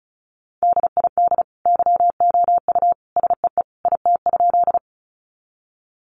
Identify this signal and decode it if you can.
Morse code, heard as a keyed tone, about 34 wpm, 700 Hz. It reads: DID YOU SEE IT?
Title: Sentences from Top 100 Words 35wpm